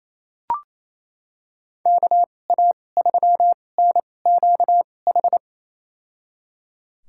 Morse code, heard as a keyed tone, about 28 wpm, spaced as for 20 wpm, 700 Hz. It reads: KA3NQH
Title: USA Call Signs - 2x Character Spacing - 28wpm